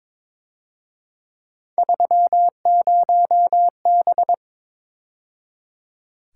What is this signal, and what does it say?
Morse code, heard as a keyed tone, about 26 wpm, 700 Hz. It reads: 30B
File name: Speed-Racing - Three Letter-Number Combinations - 22wpm